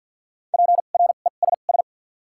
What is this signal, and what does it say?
Morse code, heard as a keyed tone, about 52 wpm, spaced as for 37 wpm, 700 Hz. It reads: PRESS
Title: 2x Character Spacing - Top 500 Words - 50wpm